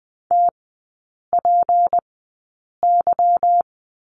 Morse code, heard as a keyed tone, about 20 wpm, spaced as for 8 wpm, 700 Hz. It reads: TPY
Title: 152 Lesson 25e - Letter p - Three Characters 20wpm